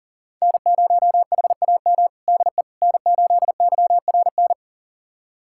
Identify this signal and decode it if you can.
Morse code, heard as a keyed tone, about 40 wpm, 700 Hz. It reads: N0HAM DE N8YRN
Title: USA Callsign DE Callsign - 5 Characters - 40wpm